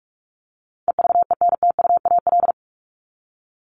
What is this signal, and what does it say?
Morse code, heard as a keyed tone, about 44 wpm, 700 Hz. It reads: EVENTUAL
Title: LY Suffix - Top 100 Words with Base Words 45wpm